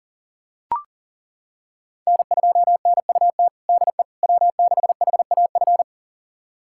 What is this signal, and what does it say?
Morse code, heard as a keyed tone, about 40 wpm, 700 Hz. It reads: N2NUT DE W6HAF
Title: USA Callsign DE Callsign - 5 Characters - 40wpm